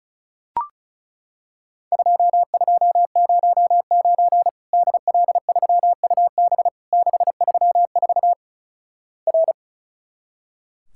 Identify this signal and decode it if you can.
Morse code, heard as a keyed tone, about 36 wpm, 700 Hz. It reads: 2209 DL3UB 634 R
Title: Worked All Europe DX Contest - QTC Exchange - 35wpm